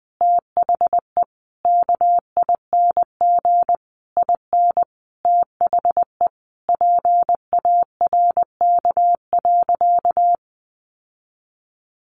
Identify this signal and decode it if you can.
Morse code, heard as a keyed tone, about 20 wpm, 700 Hz. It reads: THE KING IN THE PARK.
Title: Sentences from Top 1300 Words 20wpm